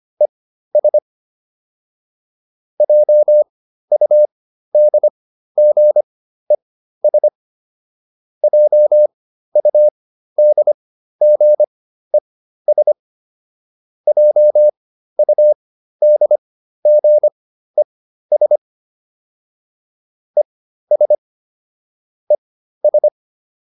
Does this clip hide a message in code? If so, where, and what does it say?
Yes: Morse code from 0.2 to 23.1 s, heard as a keyed tone, about 25 wpm, spaced as for 13 wpm, 600 Hz. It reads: ES JUDGES JUDGES JUDGES ES ES